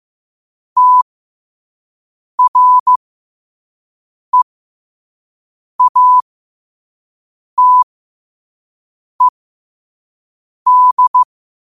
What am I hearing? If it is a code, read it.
Morse code, heard as a keyed tone, about 15 wpm, spaced as for 5 wpm, 1000 Hz. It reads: TREATED
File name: qotdfiles / 5wpm